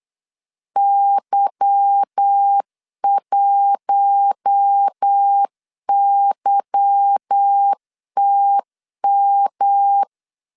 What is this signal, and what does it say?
Morse code, heard as a keyed tone, about 8 wpm, 800 Hz. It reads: Y1YTM
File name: LEZIONI / 10 SIGNS / 10B INT 8wpm 3534